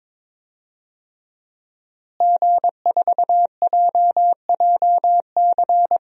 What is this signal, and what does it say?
Morse code, heard as a keyed tone, about 21 wpm, 700 Hz. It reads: G4JJC
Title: DXCC-100 Call Signs - 5 Characters - Repeated 2x - 22wpm